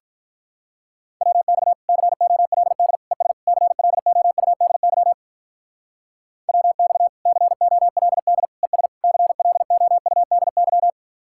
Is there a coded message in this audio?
Yes: Morse code from 1.2 to 10.9 s, heard as a keyed tone, about 45 wpm, 700 Hz. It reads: WX COLD ES CLOUDY WX COLD ES CLOUDY